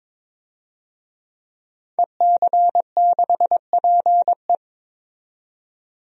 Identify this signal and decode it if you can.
Morse code, heard as a keyed tone, about 22 wpm, 700 Hz. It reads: EC6PE
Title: DXCC-100 Call Signs - 5 Characters - 22wpm